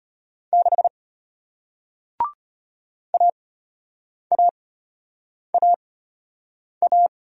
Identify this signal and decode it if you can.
Morse code, heard as a keyed tone, about 34 wpm, spaced as for 8 wpm, 700 Hz. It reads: 6 AAAA